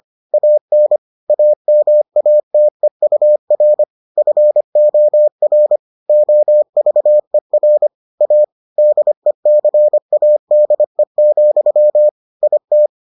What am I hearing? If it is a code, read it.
Morse code, heard as a keyed tone, about 25 wpm, 600 Hz. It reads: AN AMATEUR FOR OVER A DECADE, IT